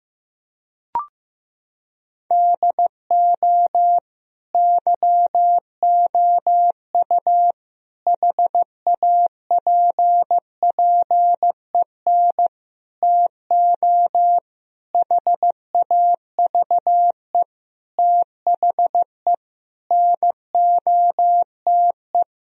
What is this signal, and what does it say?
Morse code, heard as a keyed tone, about 16 wpm, 700 Hz. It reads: DO YOU HAPPEN TO HAVE THE NOTE
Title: Sentences from Top 1000 Words - Review Part 2 - 15wpm